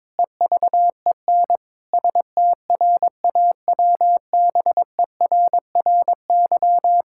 Morse code, heard as a keyed tone, about 22 wpm, 700 Hz. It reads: EVEN STRAWBERRY
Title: Pangram Challenge v2 - 22wpm